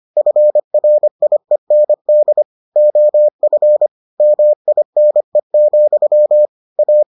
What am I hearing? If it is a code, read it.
Morse code, heard as a keyed tone, about 25 wpm, 600 Hz. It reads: FRIEND OF MINE, A